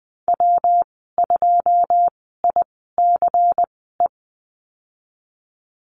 Morse code, heard as a keyed tone, about 20 wpm, spaced as for 15 wpm, 700 Hz. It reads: W2ICE
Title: 147 Lesson 24g - Letter y - Call Signs 20wpm